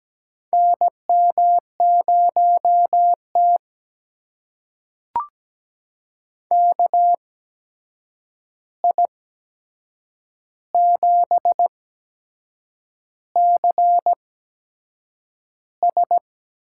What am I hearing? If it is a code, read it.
Morse code, heard as a keyed tone, about 17 wpm, spaced as for 7 wpm, 700 Hz. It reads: NM0T KI7CS
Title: USA Call Signs - 8x Character Spacing - 17wpm